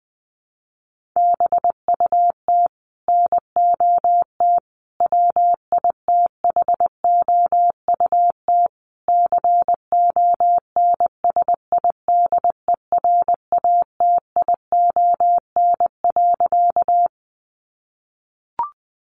Morse code, heard as a keyed tone, about 20 wpm, 700 Hz. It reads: BUT NOT WITHOUT CONSIDERATION.
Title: Sentences from Top 1500 Words 20wpm